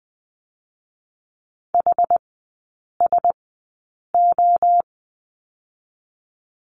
Morse code, heard as a keyed tone, about 20 wpm, spaced as for 8 wpm, 700 Hz. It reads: HSO
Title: 060 Lesson 11e - Letter h - Three Characters 20wpm